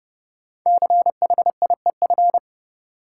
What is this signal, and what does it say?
Morse code, heard as a keyed tone, about 30 wpm, 700 Hz. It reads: CHIEF
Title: Sets of 5 Words Encore - New Word Drill - 30wpm